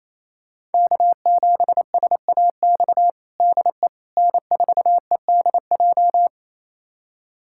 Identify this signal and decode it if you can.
Morse code, heard as a keyed tone, about 28 wpm, 700 Hz. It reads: K7SAX DE N4EDJ
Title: USA Callsign DE Callsign - 5 Characters - 28wpm